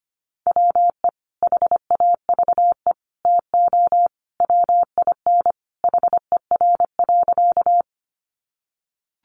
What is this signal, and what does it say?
Morse code, heard as a keyed tone, about 25 wpm, 700 Hz. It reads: WE HAVE TO WIN HER.
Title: Sentences from Top 1500 Words 25wpm